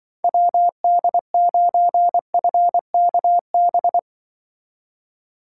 Morse code, heard as a keyed tone, about 24 wpm, 700 Hz. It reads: WD9FKB
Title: Speed Racing - USA 2x3 Call Signs 20wpm